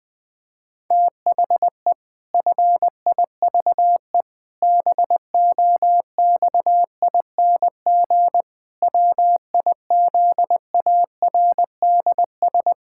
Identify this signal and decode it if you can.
Morse code, heard as a keyed tone, about 20 wpm, 700 Hz. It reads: THE FIVE BOXING WIZARDS
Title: Pangram Challenge v2 - 20wpm